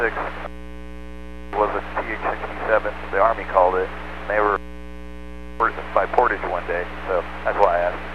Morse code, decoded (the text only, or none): none